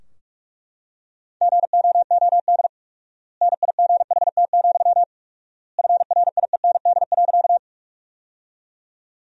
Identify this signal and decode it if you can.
Morse code, heard as a keyed tone, about 43 wpm, 700 Hz. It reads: GOOD NIGHT, FRIEND.
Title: 4x Word-Spacing - Sentences from Top 300 Words - 45wpm